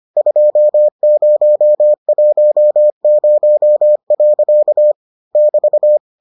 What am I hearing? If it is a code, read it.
Morse code, heard as a keyed tone, about 25 wpm, 600 Hz. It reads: 2010. <BT>